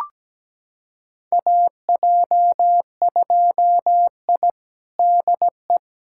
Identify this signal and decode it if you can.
Morse code, heard as a keyed tone, about 17 wpm, 700 Hz. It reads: AJ2I DE
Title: USA Callsign DE Callsign - 4 Characters - 17wpm